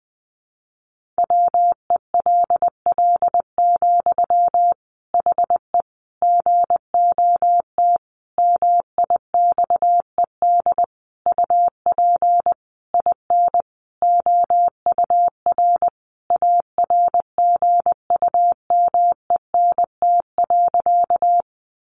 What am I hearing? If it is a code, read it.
Morse code, heard as a keyed tone, about 19 wpm, 700 Hz. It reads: WELL, HE GOT MIXED UP IN OUR ARGUMENT.
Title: Sentences from Top 1500 Words 20wpm